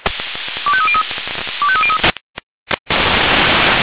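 Five-tone selective call (ZVEI), heard as quick runs of five tones, tones of about 70 ms. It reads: 25303 25303
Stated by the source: ZVEI five-tone selective call received by SDR.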